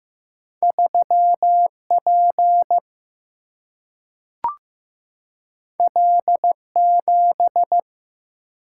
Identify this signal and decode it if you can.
Morse code, heard as a keyed tone, about 15 wpm, 700 Hz. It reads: 3 P L 7
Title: Single-Letter Single-Number 15wpm